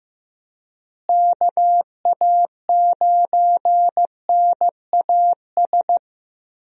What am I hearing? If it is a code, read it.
Morse code, heard as a keyed tone, about 15 wpm, 700 Hz. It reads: KA9NAS
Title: USA Call Signs - 6 Characters - Repeated 2x - 15wpm